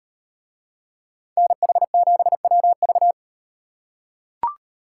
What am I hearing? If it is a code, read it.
Morse code, heard as a keyed tone, about 38 wpm, 700 Hz. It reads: NH7WV